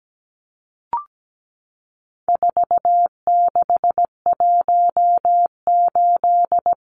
Morse code, heard as a keyed tone, about 17 wpm, 700 Hz. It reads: 4618
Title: Numbers - 4-Digit 17wpm